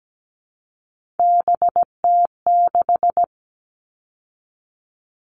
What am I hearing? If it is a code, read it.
Morse code, heard as a keyed tone, about 17 wpm, 700 Hz. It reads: BT6
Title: All Cut Numbers - 3-Digit 17wpm